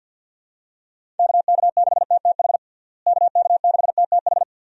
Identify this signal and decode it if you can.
Morse code, heard as a keyed tone, about 52 wpm, 700 Hz. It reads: KK6TTH KK6TTH
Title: USA Call Signs - 6 Characters - Repeated 2x - 50wpm